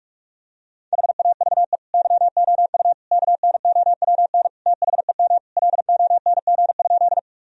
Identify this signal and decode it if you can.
Morse code, heard as a keyed tone, about 41 wpm, 700 Hz. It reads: HAVE YOU KNOWN THEM LONG?
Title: Sentences from Top 1000 Words - Review Part 3 - 45wpm